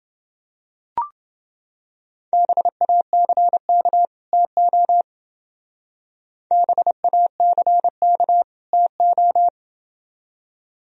Morse code, heard as a keyed tone, about 28 wpm, 700 Hz. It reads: BACK TO BACK TO